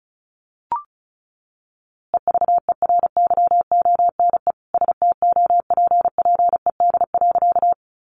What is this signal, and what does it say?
Morse code, heard as a keyed tone, about 34 wpm, 700 Hz. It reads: EVERYONE STOPPED.
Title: A Sound Of Thunder 35wpm